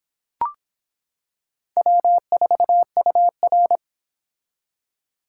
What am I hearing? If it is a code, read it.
Morse code, heard as a keyed tone, about 24 wpm, 700 Hz. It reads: W4UR